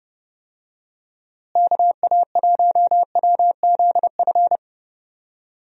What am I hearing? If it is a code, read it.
Morse code, heard as a keyed tone, about 28 wpm, 700 Hz. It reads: KA1WZF
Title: USA Call Signs - 6 Characters - Repeated 3x - 30wpm